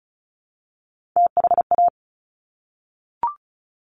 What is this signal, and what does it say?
Morse code, heard as a keyed tone, about 36 wpm, 700 Hz. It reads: THA
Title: Top Three Letter Combinations in Words 35wpm